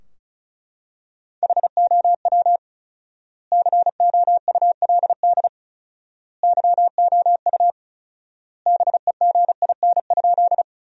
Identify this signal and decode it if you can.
Morse code, heard as a keyed tone, about 34 wpm, 700 Hz. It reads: HOW COULD YOU BEGIN?
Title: 4x Word-Spacing - Sentences from Top 300 Words - 35wpm